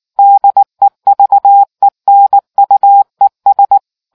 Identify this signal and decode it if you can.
Morse code, heard as a keyed tone, about 19 wpm, 800 Hz. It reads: DEVENUES